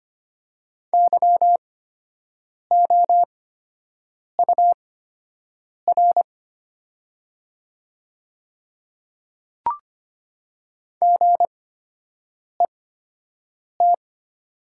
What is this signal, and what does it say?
Morse code, heard as a keyed tone, about 25 wpm, spaced as for 7 wpm, 700 Hz. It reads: YOUR GET
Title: 8x Character Spacing - Top 200 Words - 25wpm